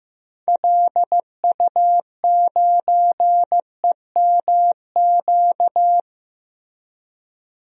Morse code, heard as a keyed tone, about 15 wpm, 700 Hz. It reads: LU9EMQ
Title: DXCC-100 Call Signs - 6 Characters - Repeated 2x - 15wpm